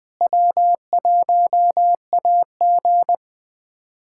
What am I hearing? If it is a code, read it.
Morse code, heard as a keyed tone, about 21 wpm, 700 Hz. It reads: W1AG
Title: USA Callsign DE Callsign - 4 Characters - 20wpm